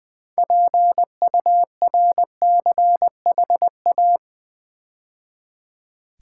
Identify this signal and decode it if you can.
Morse code, heard as a keyed tone, about 20 wpm, 700 Hz. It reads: PURCHA
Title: Build A Word - 3 Letters At A Time - 8-Letter Words - 20wpm